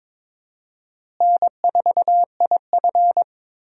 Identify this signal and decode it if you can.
Morse code, heard as a keyed tone, about 22 wpm, 700 Hz. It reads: N4IF
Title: USA Call Signs - 4 Characters - Repeated 2x - 22wpm